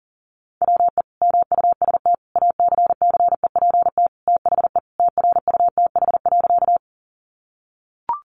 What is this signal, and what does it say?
Morse code, heard as a keyed tone, about 40 wpm, 700 Hz. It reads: WE MUST ACCEPT THE TRUTH.
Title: Sentences from Top 1300 Words 40wpm